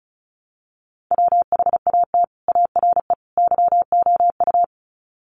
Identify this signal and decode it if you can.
Morse code, heard as a keyed tone, about 34 wpm, 700 Hz. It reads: WHAT ARE YOU
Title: Sets of 3 Words 35wpm